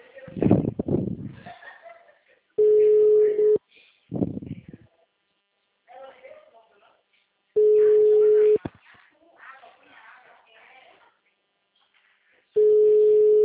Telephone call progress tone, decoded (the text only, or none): ringback tone